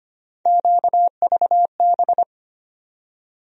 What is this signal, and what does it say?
Morse code, heard as a keyed tone, about 25 wpm, 700 Hz. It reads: QVB
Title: USA Call Sign Postfixes - 3 Letters 25wpm